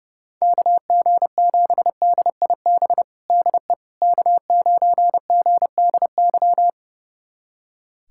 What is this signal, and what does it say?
Morse code, heard as a keyed tone, about 32 wpm, 700 Hz. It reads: KG7DIB DE K9GDY
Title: USA Callsign DE Callsign 30wpm